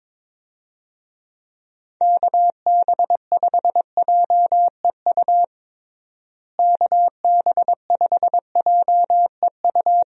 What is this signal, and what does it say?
Morse code, heard as a keyed tone, about 22 wpm, 700 Hz. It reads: KB5JEU KB5JEU
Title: USA Call Signs - 6 Characters - Repeated 2x - 22wpm